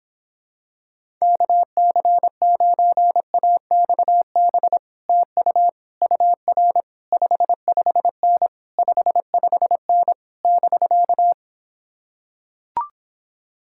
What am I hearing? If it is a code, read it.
Morse code, heard as a keyed tone, about 26 wpm, 700 Hz. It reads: KC9AXB TU UR 55N 55N <BK>